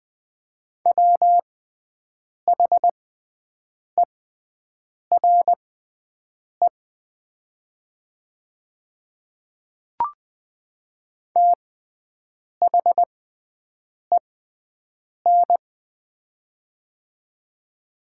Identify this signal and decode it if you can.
Morse code, heard as a keyed tone, about 20 wpm, spaced as for 7 wpm, 700 Hz. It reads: WHERE THEN
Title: 6x Character Spacing - Top 300 Words - 20wpm